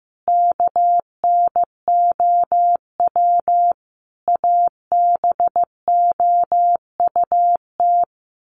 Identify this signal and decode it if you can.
Morse code, heard as a keyed tone, about 15 wpm, 700 Hz. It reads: KNOW ABOUT
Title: Sets of 5 Words 15wpm